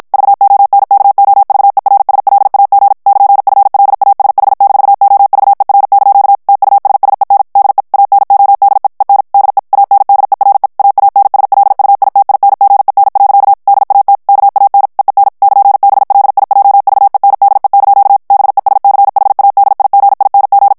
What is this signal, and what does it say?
Morse code, heard as a keyed tone, about 56 wpm, 800 Hz. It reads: VONKOVERSLAG <KN>FLASH-OVER) TUSSEN DE ANODE EN DE ANDERE AANSLUITINGEN. DAT KAN EEN <KN>BLIJVENDE) BESCHADIGING
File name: morse mp3 / HOLLANDAIS texte clair 2 vitesse 50 mots minute